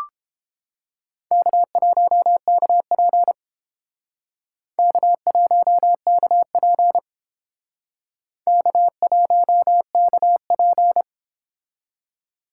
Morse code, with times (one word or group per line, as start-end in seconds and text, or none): 1.3-3.3 K1KP
4.8-7.0 K1KP
8.5-11.0 K1KP